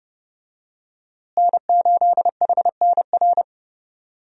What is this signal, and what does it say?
Morse code, heard as a keyed tone, about 32 wpm, 700 Hz. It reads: N8HNR
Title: USA Call Signs - 5 Characters - Repeated 3x - 30wpm